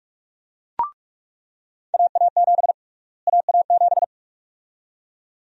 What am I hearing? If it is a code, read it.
Morse code, heard as a keyed tone, about 47 wpm, 700 Hz. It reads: AA7 AA7
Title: USA Call Sign Prefixes - Repeated 2x - 45wpm